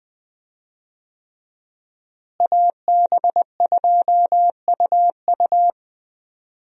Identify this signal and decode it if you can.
Morse code, heard as a keyed tone, about 20 wpm, 700 Hz. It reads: AB2UU
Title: USA Call Signs - 5 Characters - Repeated 3x - 20wpm